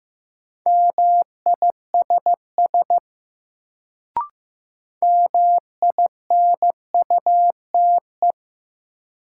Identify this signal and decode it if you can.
Morse code, heard as a keyed tone, about 15 wpm, 700 Hz. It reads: MISS MINUTE
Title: Sentences from Top 500 Words - New Word Drill - 15wpm